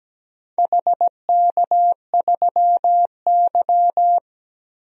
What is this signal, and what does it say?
Morse code, heard as a keyed tone, about 17 wpm, 700 Hz. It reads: HK3Y
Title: DXCC-100 Call Signs - 4 Characters - 17wpm